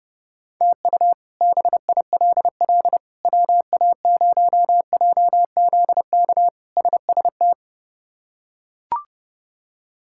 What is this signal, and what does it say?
Morse code, heard as a keyed tone, about 30 wpm, 700 Hz. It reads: TU BILL WA0JZK SST
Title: K1USN SST QSO Practice - 30wpm